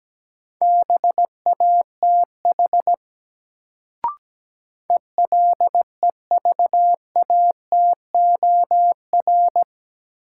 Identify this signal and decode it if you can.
Morse code, heard as a keyed tone, about 17 wpm, 700 Hz. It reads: BATH ELEVATOR
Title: Sentences from Top 1600 Words - New Word Drill Alt - 17wpm